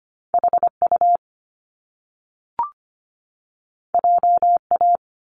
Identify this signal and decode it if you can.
Morse code, heard as a keyed tone, about 25 wpm, 700 Hz. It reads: HU JA
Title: Top Two Letter Combinations in Words 25wpm